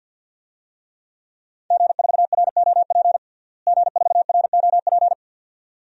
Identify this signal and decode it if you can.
Morse code, heard as a keyed tone, about 46 wpm, 700 Hz. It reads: G4ROP G4ROP
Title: DXCC-100 Call Signs - 5 Characters - Repeated 2x - 50wpm